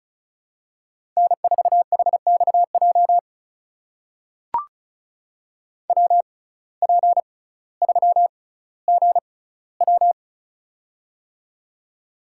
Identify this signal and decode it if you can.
Morse code, heard as a keyed tone, about 34 wpm, spaced as for 18 wpm, 700 Hz. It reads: N4HXJ WP3GW
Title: USA Call Signs - 5 Characters - 6x Character Spacing - 35wpm